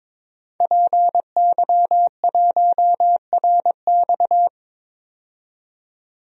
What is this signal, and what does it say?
Morse code, heard as a keyed tone, about 21 wpm, 700 Hz. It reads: PY1RX
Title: DXCC-100 Call Signs - 5 Characters - 22wpm